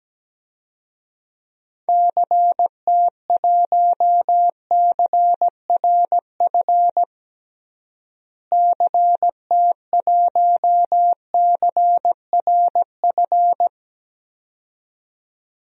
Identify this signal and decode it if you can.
Morse code, heard as a keyed tone, about 17 wpm, 700 Hz. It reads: CT1CRF CT1CRF
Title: DXCC-100 Call Signs - 6 Characters - Repeated 2x - 17wpm